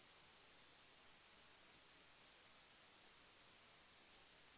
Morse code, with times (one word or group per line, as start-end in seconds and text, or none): none